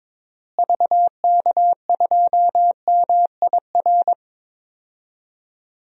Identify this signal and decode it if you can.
Morse code, heard as a keyed tone, about 22 wpm, 700 Hz. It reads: VK2MIR
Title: DXCC-100 Call Signs - 6 Characters - Repeated 2x - 22wpm